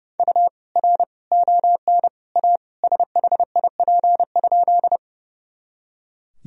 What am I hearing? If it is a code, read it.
Morse code, heard as a keyed tone, about 32 wpm, 700 Hz. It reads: U R ON A SHIP?
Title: Sentences from Top 400 Words and Abbreviations 30wpm